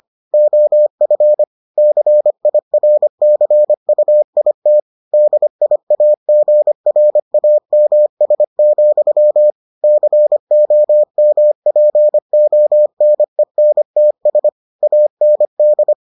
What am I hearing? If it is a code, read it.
Morse code, heard as a keyed tone, about 25 wpm, 600 Hz. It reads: OF CIRCUIT DIAGRAMS, COMPONENTS AND